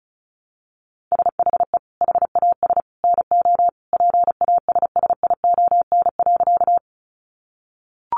Morse code, heard as a keyed tone, about 34 wpm, 700 Hz. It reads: SHE HAS NO PASSION.
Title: Sentences from Top 1300 Words 35wpm